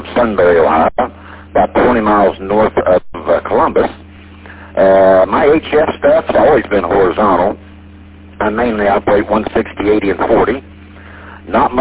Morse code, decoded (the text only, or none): none